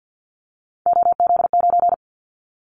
Morse code, heard as a keyed tone, about 50 wpm, 700 Hz. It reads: OZ9
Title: Callsign Prefixes for Top 50 DXCC - 50wpm